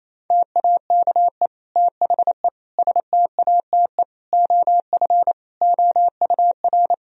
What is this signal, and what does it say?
Morse code, heard as a keyed tone, about 28 wpm, 700 Hz. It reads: TAKE THE STATE OF OUR
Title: Sentences from Top 1000 Words - Review Part 3 - 28wpm